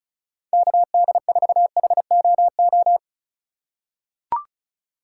Morse code, heard as a keyed tone, about 36 wpm, 700 Hz. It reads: KD4HOO
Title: USA Call Signs - 6 Characters - 4x Character Spacing - 35wpm